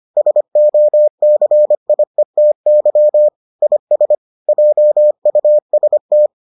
Morse code, heard as a keyed tone, about 25 wpm, 600 Hz. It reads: SOCIETY IS JUST